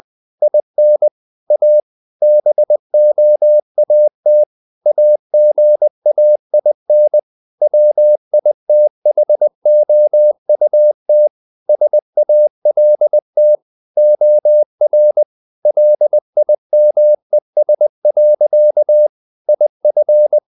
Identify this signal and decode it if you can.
Morse code, heard as a keyed tone, about 20 wpm, 600 Hz. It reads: IN A BOAT AGAIN WITHOUT SALT OR LIMES. IF
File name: the old man and the sea part 2 20wpm